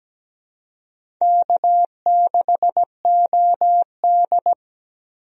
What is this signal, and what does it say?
Morse code, heard as a keyed tone, about 17 wpm, 700 Hz. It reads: K6OD
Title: USA Call Signs - 4 Characters - 17wpm